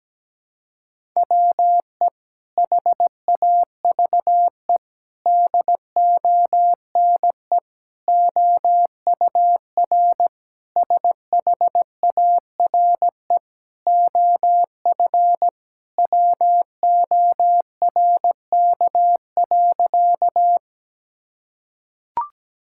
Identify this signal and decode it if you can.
Morse code, heard as a keyed tone, about 17 wpm, 700 Hz. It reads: WE HAVE DONE OUR SHARE OF WORK.
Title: Sentences from Top 1000 Words - Review Part 6 - 17wpm